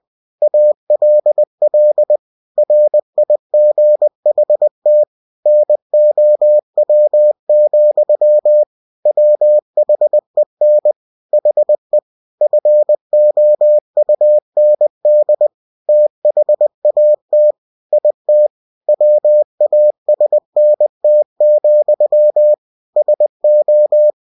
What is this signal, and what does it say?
Morse code, heard as a keyed tone, about 20 wpm, 600 Hz. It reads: ALL RIGHT NOW, WHEN HE FOUND THAT IT WASNT, SO
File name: cw-books / winnie-the-pooh / winnie-20wpm-page-0100